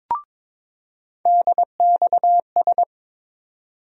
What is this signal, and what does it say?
Morse code, heard as a keyed tone, about 22 wpm, 700 Hz. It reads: DXS
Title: USA Call Sign Postfixes - 3 Letters 22wpm